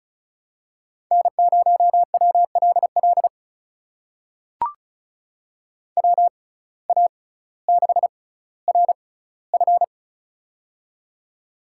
Morse code, heard as a keyed tone, about 36 wpm, spaced as for 18 wpm, 700 Hz. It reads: N0WLL WA6RF
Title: USA Call Signs - 5 Characters - 6x Character Spacing - 35wpm